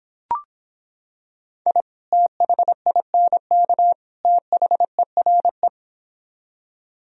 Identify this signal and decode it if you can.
Morse code, heard as a keyed tone, about 26 wpm, 700 Hz. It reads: I THINK THERE